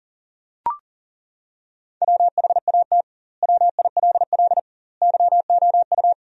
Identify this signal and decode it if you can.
Morse code, heard as a keyed tone, about 40 wpm, 700 Hz. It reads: WHAT WILL YOU
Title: 2x Word-Spacing - Sentences from Top 100 Words - 40wpm